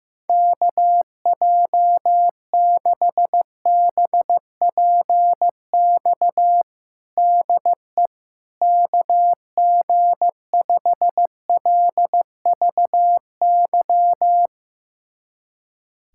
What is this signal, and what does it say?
Morse code, heard as a keyed tone, about 15 wpm, 700 Hz. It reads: KJ6BPX DE KG5LVY
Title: USA Callsign DE Callsign - 6 Characters - 15wpm